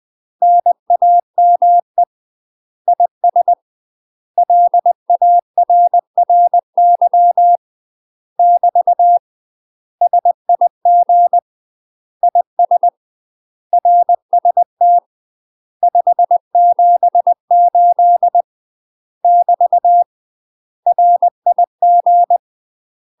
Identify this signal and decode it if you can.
Morse code, heard as a keyed tone, about 20 wpm, 700 Hz. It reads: NAME IS LARRY <BT> SIG IS RST 578 <BT> RIG
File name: UK MASTER QSOs TRAINING / 20WPM / UK MASTER QSOs 0072